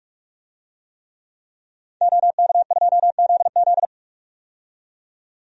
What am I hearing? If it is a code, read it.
Morse code, heard as a keyed tone, about 42 wpm, 700 Hz. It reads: OK2ZZ
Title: DXCC-100 Call Signs - 5 Characters - 45wpm